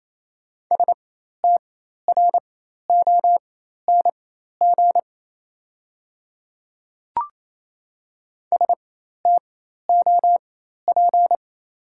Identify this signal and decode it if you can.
Morse code, heard as a keyed tone, about 28 wpm, spaced as for 13 wpm, 700 Hz. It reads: STRONG STOP
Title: 4x Character Spacing - Top 500 Words - 28wpm